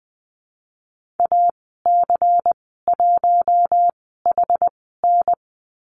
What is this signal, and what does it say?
Morse code, heard as a keyed tone, about 20 wpm, spaced as for 15 wpm, 700 Hz. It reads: AC1HN
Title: 154 Lesson 25g - Letter p - Call Signs 20wpm